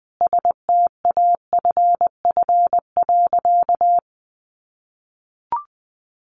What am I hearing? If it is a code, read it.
Morse code, heard as a keyed tone, about 20 wpm, 700 Hz. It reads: STAFF.
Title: Sentences from Top 1200 Words 20wpm